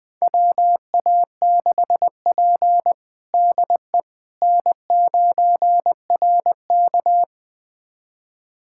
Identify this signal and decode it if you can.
Morse code, heard as a keyed tone, about 20 wpm, 700 Hz. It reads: WA6P DE N9RK
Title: USA Callsign DE Callsign - 4 Characters - 20wpm